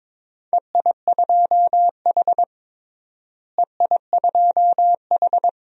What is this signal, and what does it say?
Morse code, heard as a keyed tone, about 22 wpm, 700 Hz. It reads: EI2H EI2H
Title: DXCC-100 Call Signs - 4 Characters - Repeated 2x - 22wpm